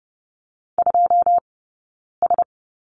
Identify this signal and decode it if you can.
Morse code, heard as a keyed tone, about 30 wpm, 700 Hz. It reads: 2 S
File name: Lesson 19d - Letter w - Two Characters 30wpm